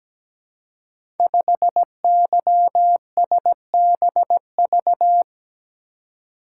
Morse code, heard as a keyed tone, about 17 wpm, 700 Hz. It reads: 5YSBV
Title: Code Groups - 5 Letter:Numbers - 17wpm